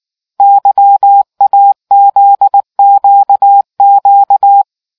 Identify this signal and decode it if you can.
Morse code, heard as a keyed tone, about 19 wpm, 800 Hz. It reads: YAZQQ